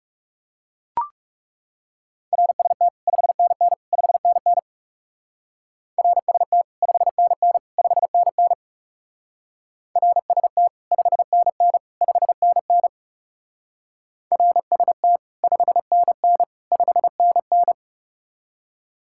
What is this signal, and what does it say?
Morse code, heard as a keyed tone, about 38 wpm, 700 Hz. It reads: RST 5NN 5NN RST 5NN 5NN RST 5NN 5NN RST 5NN 5NN